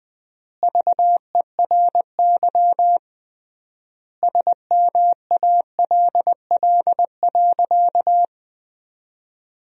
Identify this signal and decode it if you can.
Morse code, heard as a keyed tone, about 20 wpm, 700 Hz. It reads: VERY SMALL.
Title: 3x Word-Spacing - Sentences from Top 400 Words - 20wpm